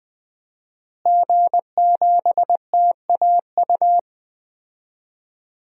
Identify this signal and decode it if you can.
Morse code, heard as a keyed tone, about 20 wpm, 700 Hz. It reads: G7TAU
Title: DXCC-100 Call Signs - 5 Characters - Repeated 2x - 20wpm